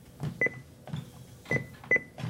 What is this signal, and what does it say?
There is no code.